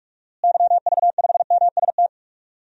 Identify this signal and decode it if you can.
Morse code, heard as a keyed tone, about 43 wpm, 700 Hz. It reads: YV5MST
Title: DXCC-100 Call Signs - 6 Characters - Repeated 2x - 45wpm